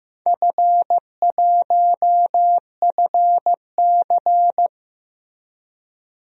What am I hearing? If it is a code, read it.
Morse code, heard as a keyed tone, about 15 wpm, 700 Hz. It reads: F1FC
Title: DXCC-100 Call Signs - 4 Characters - Repeated 2x - 15wpm